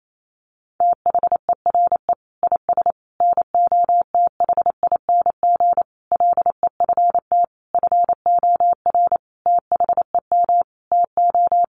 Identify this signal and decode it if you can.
Morse code, heard as a keyed tone, about 28 wpm, 700 Hz. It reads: THERE IS NOTHING LEFT FOR THEM TO
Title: Sentences from Top 1100 Words 28wpm